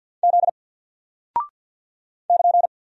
Morse code, heard as a keyed tone, about 48 wpm, 700 Hz. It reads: Z <KN>
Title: Single Character (Letter:Number:Punctuation:Prosign) 50wpm